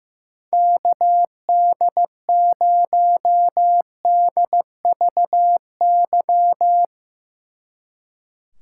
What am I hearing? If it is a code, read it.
Morse code, heard as a keyed tone, about 16 wpm, 700 Hz. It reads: KD0DVY
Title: Speed Racing - USA 2x3 Call Signs 15wpm